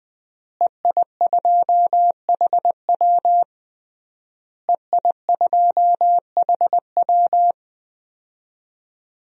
Morse code, heard as a keyed tone, about 20 wpm, 700 Hz. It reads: EI2HW EI2HW
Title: DXCC-100 Call Signs - 5 Characters - Repeated 3x - 20wpm